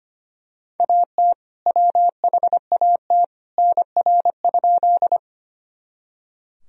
Morse code, heard as a keyed tone, about 25 wpm, 700 Hz. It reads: AT WHAT NR?
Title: Sentences from Top 100 Words and Abbreviations 25wpm